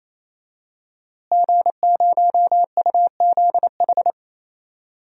Morse code, heard as a keyed tone, about 28 wpm, 700 Hz. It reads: G0UZH
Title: DXCC-100 Call Signs - 5 Characters - Repeated 3x - 28wpm